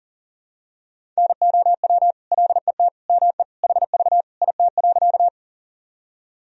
Morse code, heard as a keyed tone, about 40 wpm, 700 Hz. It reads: NOW LET ME HV IT.
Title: Sentences from Top 300 Words and Abbreviations 40wpm